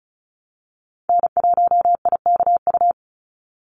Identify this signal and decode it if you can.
Morse code, heard as a keyed tone, about 34 wpm, 700 Hz. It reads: N1IKU
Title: USA Call Signs 35wpm